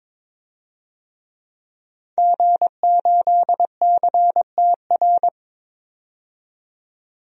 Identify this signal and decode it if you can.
Morse code, heard as a keyed tone, about 22 wpm, 700 Hz. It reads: G8CTR
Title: DXCC-100 Call Signs - 5 Characters - 22wpm